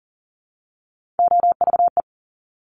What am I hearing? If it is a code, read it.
Morse code, heard as a keyed tone, about 39 wpm, 690 Hz. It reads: OVE
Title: Top Three Letter Combinations in Words 40wpm